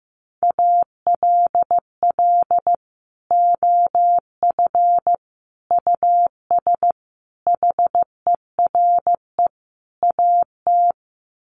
Morse code, heard as a keyed tone, about 15 wpm, 700 Hz. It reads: ALL OF US HERE AT
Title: Sets of 5 Words 15wpm